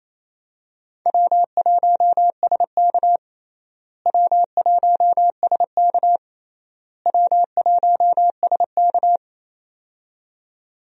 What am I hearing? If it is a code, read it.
Morse code, heard as a keyed tone, about 28 wpm, 700 Hz. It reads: W1SK W1SK W1SK
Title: USA Call Signs - 4 Characters - Repeated 3x - 28wpm